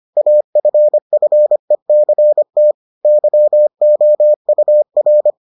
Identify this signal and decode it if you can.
Morse code, heard as a keyed tone, about 25 wpm, 600 Hz. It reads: AFFECT YOUR